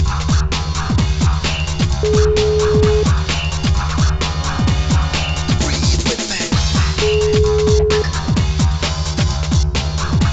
Ringback tone (Europe, 425 Hz), with noise.